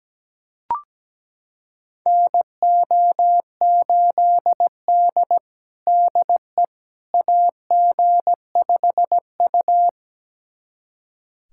Morse code, heard as a keyed tone, about 17 wpm, 700 Hz. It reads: NO8D DE AG5U
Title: USA Callsign DE Callsign - 4 Characters - 17wpm